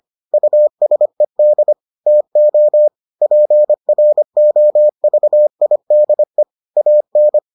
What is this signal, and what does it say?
Morse code, heard as a keyed tone, about 25 wpm, 600 Hz. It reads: USED TO PROVIDE AN